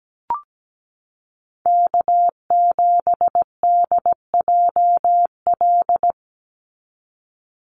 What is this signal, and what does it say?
Morse code, heard as a keyed tone, about 17 wpm, 700 Hz. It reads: K7DJL
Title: NA SOTA Chaser Callsigns 17wpm